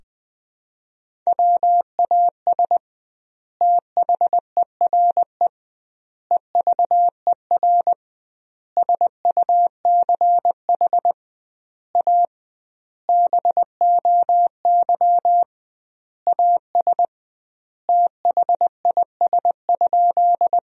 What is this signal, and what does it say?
Morse code, heard as a keyed tone, about 20 wpm, 700 Hz. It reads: WAS THERE EVER SUCH A BOY AS THIS?
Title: 2x Word-Spacing - Sentences from Top 400 Words - 20wpm